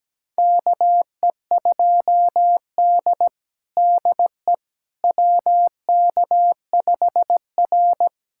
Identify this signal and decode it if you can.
Morse code, heard as a keyed tone, about 17 wpm, 700 Hz. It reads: KE2D DE WK5R
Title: USA Callsign DE Callsign - 4 Characters - 17wpm